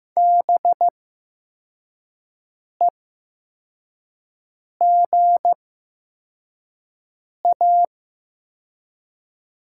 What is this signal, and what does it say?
Morse code, heard as a keyed tone, about 15 wpm, spaced as for 4 wpm, 700 Hz. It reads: BEGA
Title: 8x Character Spacing - Top 300 Words - 15wpm